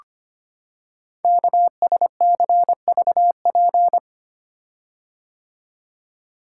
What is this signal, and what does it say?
Morse code, heard as a keyed tone, about 25 wpm, 700 Hz. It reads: KSCVP
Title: Code Groups - 5 Letters - 25wpm